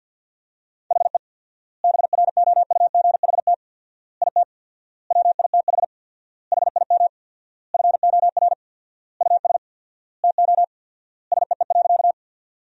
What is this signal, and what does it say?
Morse code, heard as a keyed tone, about 50 wpm, 700 Hz. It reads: HE BROUGHT IT WITH HIM FOR US TO SEE.
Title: 4x Word-Spacing - Sentences from Top 500 Words - 50wpm